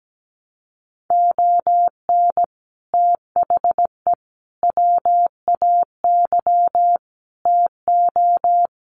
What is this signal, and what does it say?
Morse code, heard as a keyed tone, about 17 wpm, 700 Hz. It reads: ON THE WAY TO
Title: Sets of 5 Words 17wpm